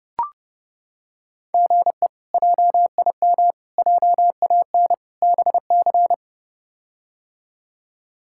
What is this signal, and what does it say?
Morse code, heard as a keyed tone, about 30 wpm, 700 Hz. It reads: GE JIM JAN BC
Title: K1USN SST QSO Practice - 30wpm